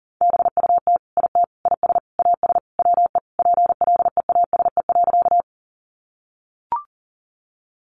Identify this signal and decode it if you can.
Morse code, heard as a keyed tone, about 37 wpm, 700 Hz. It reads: BUT IT IS AS WE PLEASE.
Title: Sentences from Top 1500 Words 40wpm